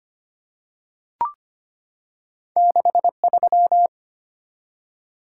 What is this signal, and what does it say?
Morse code, heard as a keyed tone, about 25 wpm, 700 Hz. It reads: 63
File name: Two Letter-Number Combinations - 25wpm